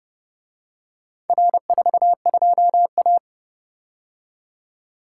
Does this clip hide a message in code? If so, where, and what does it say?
Yes: Morse code from 1.3 to 3.2 s, heard as a keyed tone, about 29 wpm, 700 Hz. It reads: R42A